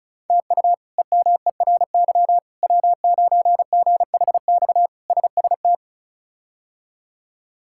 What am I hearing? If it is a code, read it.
Morse code, heard as a keyed tone, about 34 wpm, 700 Hz. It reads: TU EMERY W9GHX SST
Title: K1USN SST QSO Practice - 35wpm